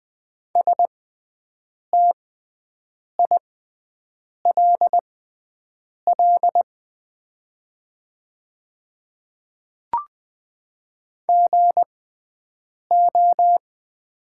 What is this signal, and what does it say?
Morse code, heard as a keyed tone, about 20 wpm, spaced as for 7 wpm, 690 Hz. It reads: STILL GO
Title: 6x Character Spacing - Top 500 Words - 20wpm